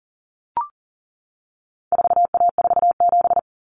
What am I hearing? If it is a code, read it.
Morse code, heard as a keyed tone, about 40 wpm, 700 Hz. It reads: 4A47
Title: Common Cut Numbers - 4-Digit 40wpm